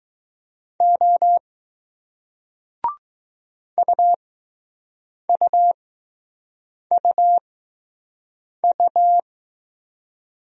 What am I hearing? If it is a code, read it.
Morse code, heard as a keyed tone, about 20 wpm, spaced as for 6 wpm, 700 Hz. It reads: O UUUU